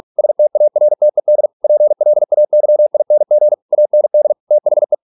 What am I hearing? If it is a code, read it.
Morse code, heard as a keyed tone, about 46 wpm, 600 Hz. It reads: STARTED PLAYING AND THE